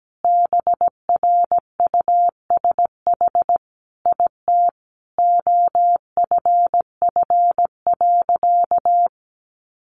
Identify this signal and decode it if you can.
Morse code, heard as a keyed tone, about 17 wpm, 700 Hz. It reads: BRUSH IT OFF.
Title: Sentences from Top 1100 Words 17wpm